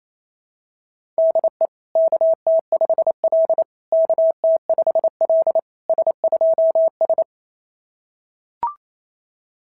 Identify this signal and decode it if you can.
Morse code, heard as a keyed tone, about 25 wpm, 650 Hz. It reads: DE KT5L KT5L S2S
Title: SOTA QSO Practice 28wpm